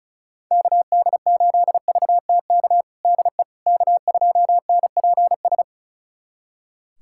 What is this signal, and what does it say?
Morse code, heard as a keyed tone, about 36 wpm, 700 Hz. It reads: KD8VTK DE K2NPS
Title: USA Callsign DE Callsign 35wpm